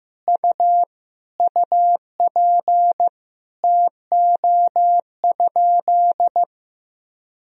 Morse code, heard as a keyed tone, about 15 wpm, 700 Hz. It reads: U UP TO?
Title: Sentences from Top 100 Words and Abbreviations 15wpm